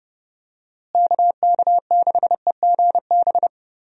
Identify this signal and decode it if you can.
Morse code, heard as a keyed tone, about 30 wpm, 700 Hz. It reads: KK6EGB
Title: USA Call Signs - 6 Characters - Repeated 3x - 30wpm